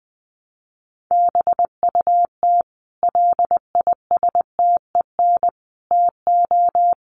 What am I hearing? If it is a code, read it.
Morse code, heard as a keyed tone, about 20 wpm, 700 Hz. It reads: BUT LISTEN TO
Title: Sentences from Top 400 Words 20wpm